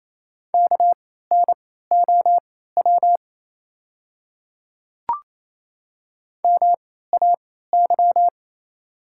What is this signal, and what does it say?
Morse code, heard as a keyed tone, about 29 wpm, spaced as for 16 wpm, 690 Hz. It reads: KNOW MAY
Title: 3x Character Spacing - Top 100 Words - 28wpm